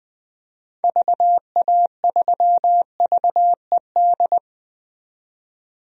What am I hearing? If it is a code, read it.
Morse code, heard as a keyed tone, about 20 wpm, 700 Hz. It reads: VA3VED
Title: DXCC-100 Call Signs - 6 Characters - 20wpm